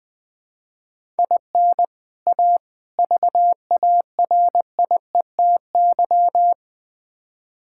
Morse code, heard as a keyed tone, about 21 wpm, 700 Hz. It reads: IN A VARIETY
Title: Build A Phrase - 4-Word Phrases - 20wpm